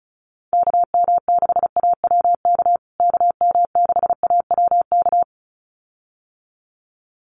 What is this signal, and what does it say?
Morse code, heard as a keyed tone, about 34 wpm, 700 Hz. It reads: KM6AWK KM6AWK
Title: USA Call Signs Repeated 2x 35wpm